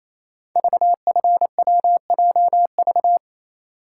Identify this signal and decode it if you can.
Morse code, heard as a keyed tone, about 28 wpm, 700 Hz. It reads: VFWJV
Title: Code Groups - 5 Letters - 28wpm